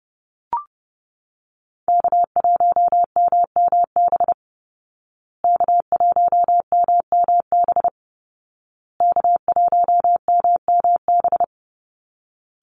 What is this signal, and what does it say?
Morse code, heard as a keyed tone, about 30 wpm, 700 Hz. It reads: K1MMB K1MMB K1MMB
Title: Call Signs Repeated 2x - 30wpm